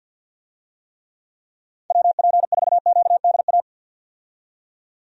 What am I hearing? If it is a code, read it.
Morse code, heard as a keyed tone, about 50 wpm, 700 Hz. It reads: WP4QDA
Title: USA Call Signs - 6 Characters - 50wpm